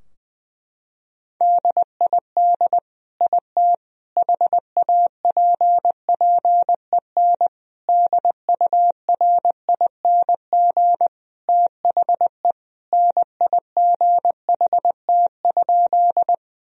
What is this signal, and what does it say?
Morse code, heard as a keyed tone, about 21 wpm, 700 Hz. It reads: DID IT HAPPEN DURING THE NIGHT?
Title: Sentences from Top 1000 Words - Review Part 2 - 20wpm